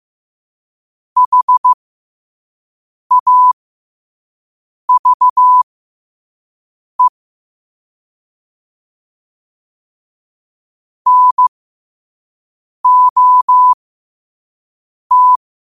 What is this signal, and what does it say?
Morse code, heard as a keyed tone, about 15 wpm, spaced as for 5 wpm, 1000 Hz. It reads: HAVE NOT
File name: qotdfiles / 5wpm